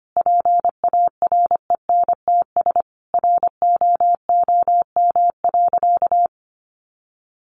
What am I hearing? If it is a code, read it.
Morse code, heard as a keyed tone, about 25 wpm, 700 Hz. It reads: PARENTS ROOM.
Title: Easy Phrases 25wpm